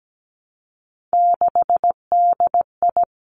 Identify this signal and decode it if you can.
Morse code, heard as a keyed tone, about 17 wpm, 700 Hz. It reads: 6DI
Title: Three Challenging Letter-Number Combinations 17wpm